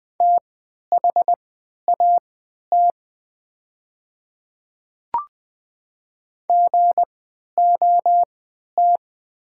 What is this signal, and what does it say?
Morse code, heard as a keyed tone, about 20 wpm, spaced as for 11 wpm, 690 Hz. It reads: THAT GOT
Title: 3x Character Spacing - Top 300 Words - 20wpm